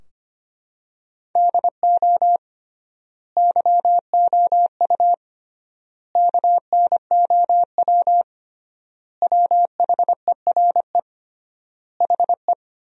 Morse code, heard as a keyed tone, about 25 wpm, 700 Hz. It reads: DO YOU KNOW WHERE HE
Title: 3x Word-Spacing - Sentences from Top 300 Words - 25wpm